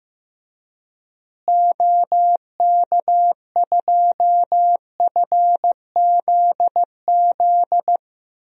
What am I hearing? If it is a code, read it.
Morse code, heard as a keyed tone, about 15 wpm, 700 Hz. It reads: OK2FZZ
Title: DXCC-100 Call Signs - 6 Characters - Repeated 3x - 15wpm